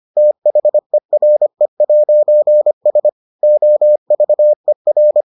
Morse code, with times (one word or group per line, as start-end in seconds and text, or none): 0.2-3.1 THERE'S
3.4-5.2 OVER